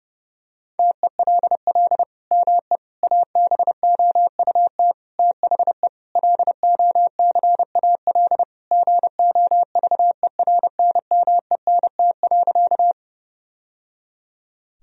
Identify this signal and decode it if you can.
Morse code, heard as a keyed tone, about 30 wpm, 700 Hz. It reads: TELL ME ABOUT THE LOCAL GOVERNMENT.
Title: Sentences from Top 1000 Words - Review Part 3 - 30wpm